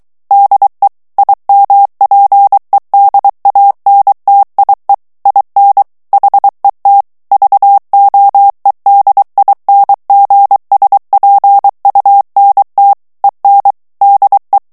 Morse code, heard as a keyed tone, about 23 wpm, 800 Hz. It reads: DE IMPEDANTIE IN HET VOEDINGSPUNT EN DE
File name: HOLLANDAIS texte clair 1 vitesse 22 mots minute